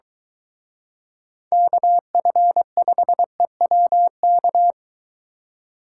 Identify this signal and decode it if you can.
Morse code, heard as a keyed tone, about 23 wpm, 700 Hz. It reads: KF5EWK